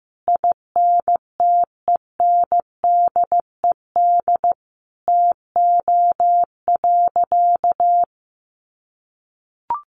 Morse code, heard as a keyed tone, about 14 wpm, 700 Hz. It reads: INTENDED TO.
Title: A Sound Of Thunder 15wpm